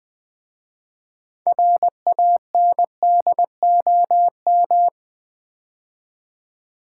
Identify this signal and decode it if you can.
Morse code, heard as a keyed tone, about 20 wpm, 700 Hz. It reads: RANDOM
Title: 6 Letter Words - First 3 Letters Given - 20wpm